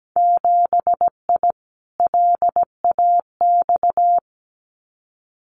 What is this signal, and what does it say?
Morse code, heard as a keyed tone, about 17 wpm, 700 Hz. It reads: 7I LAX
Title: Winter Field Day Exchanges Encore 17wpm